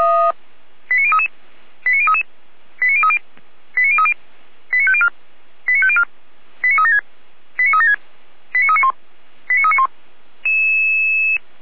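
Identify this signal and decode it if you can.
Five-tone selective call (ZVEI), heard as quick runs of five tones, tones of about 70 ms. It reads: 89033 89033 89030 89030 89504 89504 89376 89376 89391 89391
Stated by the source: Five-tone ZVEI selective call received on SDR.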